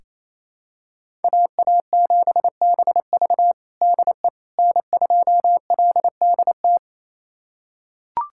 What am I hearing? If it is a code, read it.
Morse code, heard as a keyed tone, about 27 wpm, 700 Hz. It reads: AA7BV DE N2LDT
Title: USA Callsign DE Callsign - 5 Characters - 28wpm